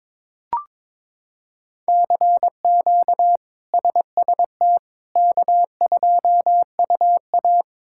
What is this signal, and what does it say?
Morse code, heard as a keyed tone, about 22 wpm, 700 Hz. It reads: CQ SST K2UA
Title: K1USN SST QSO Practice - 22wpm